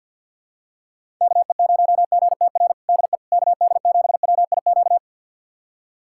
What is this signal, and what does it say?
Morse code, heard as a keyed tone, about 52 wpm, 700 Hz. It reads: KE0GTR DE KD7WIQ
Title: USA Callsign DE Callsign - 6 Characters - 50wpm